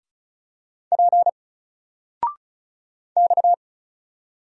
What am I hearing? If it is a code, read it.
Morse code, heard as a keyed tone, about 34 wpm, 700 Hz. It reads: P X
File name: Single-Letter-Number-35wpm